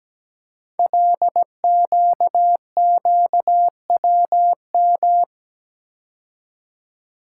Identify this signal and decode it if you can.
Morse code, heard as a keyed tone, about 17 wpm, 700 Hz. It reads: LQQWM
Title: Code Groups - 5 Letters - 17wpm